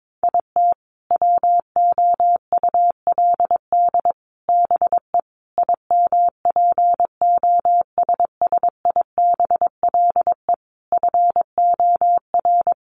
Morse code, heard as a keyed tone, about 22 wpm, 700 Hz. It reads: IT WOULD BE IMPOSSIBLE FOR
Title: Sentences from Top 1100 Words 22wpm